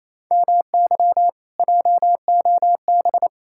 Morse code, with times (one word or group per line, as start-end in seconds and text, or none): 0.3-1.3 MY
1.6-3.3 JOB